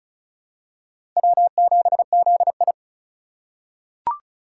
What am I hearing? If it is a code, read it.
Morse code, heard as a keyed tone, about 36 wpm, 700 Hz. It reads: W7ZI
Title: USA Call Signs - 4 Characters - 2x Character Spacing - 35wpm